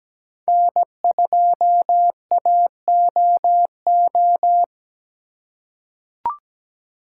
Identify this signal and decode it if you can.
Morse code, heard as a keyed tone, about 17 wpm, 700 Hz. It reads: N2AOO
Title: USA Call Signs - 5 Characters - 3x Character Spacing - 17wpm